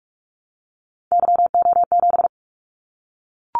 Morse code, heard as a keyed tone, about 44 wpm, 700 Hz. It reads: YO7
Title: Callsign Prefixes for Top 25 DXCC - Repeated 3x - 45wpm